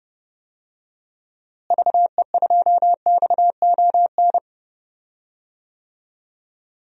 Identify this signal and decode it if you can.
Morse code, heard as a keyed tone, about 30 wpm, 700 Hz. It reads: VE2XON
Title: DXCC-100 Call Signs - 6 Characters - Repeated 2x - 30wpm